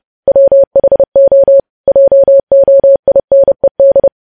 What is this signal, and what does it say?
Morse code, heard as a keyed tone, about 30 wpm, 550 Hz. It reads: WHO JOINED
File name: podcast / News headlines-1x-30x30